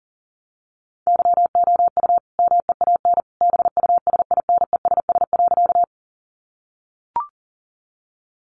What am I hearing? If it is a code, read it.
Morse code, heard as a keyed tone, about 40 wpm, 700 Hz. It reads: YOU MEAN BUSINESS.
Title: Sentences from Top 600 Words 40wpm